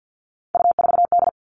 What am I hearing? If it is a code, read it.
Morse code, heard as a keyed tone, about 50 wpm, 700 Hz. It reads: U4D
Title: All Cut Numbers - 3-Digit 50wpm